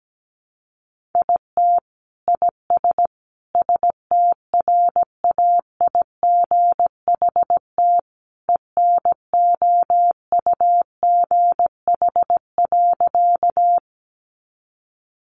Morse code, heard as a keyed tone, about 17 wpm, 700 Hz. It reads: IT IS STRAIGHT ENOUGH.
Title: Sentences from Top 1000 Words 17wpm